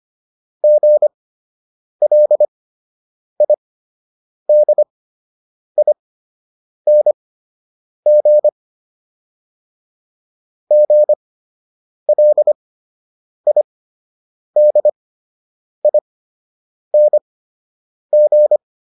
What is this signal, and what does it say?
Morse code, heard as a keyed tone, about 25 wpm, spaced as for 8 wpm, 600 Hz. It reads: GLIDING GLIDING